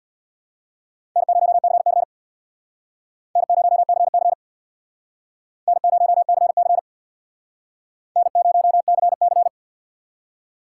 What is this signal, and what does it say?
Morse code, heard as a keyed tone, about 65 wpm, 700 Hz. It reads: N0CC N0CC N0CC N0CC